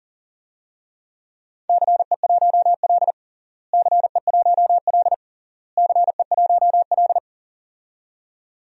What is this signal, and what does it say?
Morse code, heard as a keyed tone, about 37 wpm, 700 Hz. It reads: CE1L CE1L CE1L
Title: DXCC-100 Call Signs - 4 Characters - Repeated 3x - 40wpm